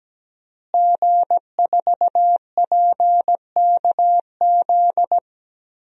Morse code, heard as a keyed tone, about 17 wpm, 700 Hz. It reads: G4PKZ
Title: DXCC-100 Call Signs - 5 Characters - Repeated 3x - 17wpm